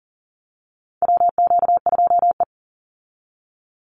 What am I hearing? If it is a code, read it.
Morse code, heard as a keyed tone, about 40 wpm, 700 Hz. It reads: WQ2E
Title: Call Signs Repeated 2x - 40wpm